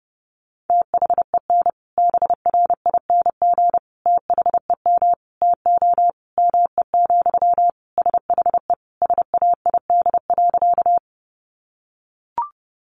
Morse code, heard as a keyed tone, about 30 wpm, 700 Hz. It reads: THEN BRING THEM TO ME, SHE SAID.
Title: Sentences from Top 500 Words 30wpm